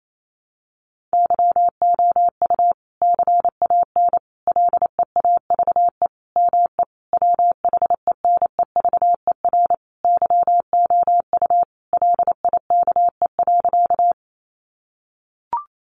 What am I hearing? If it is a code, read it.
Morse code, heard as a keyed tone, about 28 wpm, 700 Hz. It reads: YOU CAN LEAVE ME WHENEVER YOU LIKE.
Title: Sentences from Top 1200 Words 28wpm